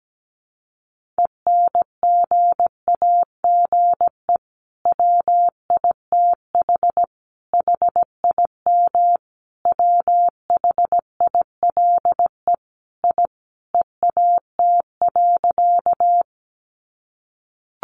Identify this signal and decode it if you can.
Morse code, heard as a keyed tone, about 17 wpm, 700 Hz. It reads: ENGAGE WITH HIM WHILE I EAT.
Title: Sentences from Top 1500 Words 17wpm